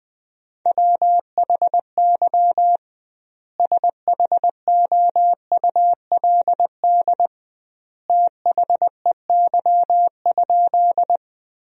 Morse code, heard as a keyed tone, about 20 wpm, 700 Hz. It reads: WHY SHOULD THEY?
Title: 2x Word-Spacing - Sentences from Top 300 Words - 20wpm